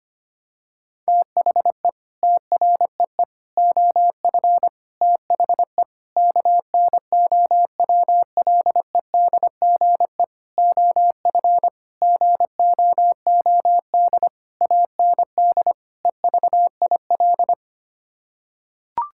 Morse code, heard as a keyed tone, about 25 wpm, 700 Hz. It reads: THE TREE OF THE KNOWLEDGE OF GOOD AND EVIL
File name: Sets of 9 Words - 25wpm